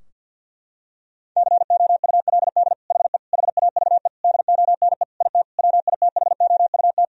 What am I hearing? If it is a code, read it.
Morse code, heard as a keyed tone, about 51 wpm, 700 Hz. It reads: COULD HE HAVE DONE IT WITHOUT
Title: Sentences from Top 1000 Words - Review Part 3 - 50wpm